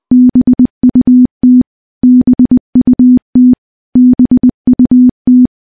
Morse code, heard as a keyed tone, about 20 wpm, 250 Hz. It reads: BUT BUT BUT
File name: morse-training-20WPM